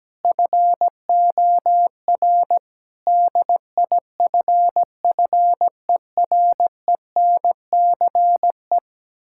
Morse code, heard as a keyed tone, about 17 wpm, 700 Hz. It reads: FOR DIFFERENCE
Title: Sentences from Top 1600 Words 17wpm